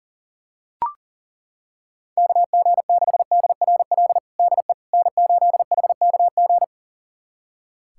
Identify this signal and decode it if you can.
Morse code, heard as a keyed tone, about 40 wpm, 700 Hz. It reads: KG6DRL DE N8HKG
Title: USA Callsign DE Callsign 40wpm